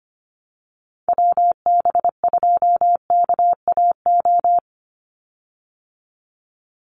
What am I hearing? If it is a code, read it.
Morse code, heard as a keyed tone, about 25 wpm, 700 Hz. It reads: WB2KAO
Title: Call Signs Repeated 3x - 25wpm